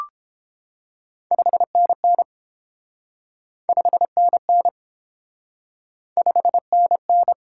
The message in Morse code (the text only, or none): 5NN 5NN 5NN